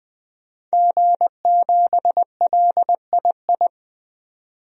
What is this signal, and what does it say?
Morse code, heard as a keyed tone, about 21 wpm, 700 Hz. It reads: G7LII
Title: DXCC-100 Call Signs - 5 Characters - 20wpm